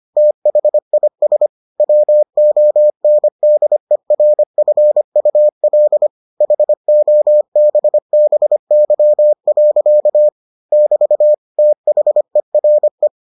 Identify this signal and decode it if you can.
Morse code, heard as a keyed tone, about 25 wpm, 600 Hz. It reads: THIS WONDERFUL HOBBY. <BT> THERE